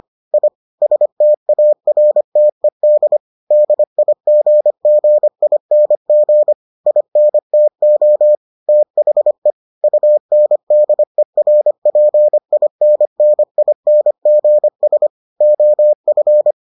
Morse code, heard as a keyed tone, about 25 wpm, 600 Hz. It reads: I STARTED DIGGING INTO THE UNDERPINNINGS OF